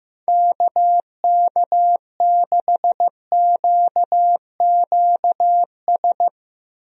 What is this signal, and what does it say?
Morse code, heard as a keyed tone, about 16 wpm, 700 Hz. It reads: KK6QQS
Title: USA Call Signs - 6 Characters - Repeated 2x - 15wpm